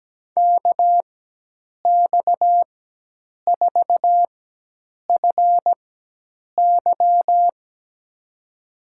Morse code, heard as a keyed tone, about 17 wpm, spaced as for 8 wpm, 700 Hz. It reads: KX4FY
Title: USA Call Signs - 4x Character Spacing - 17wpm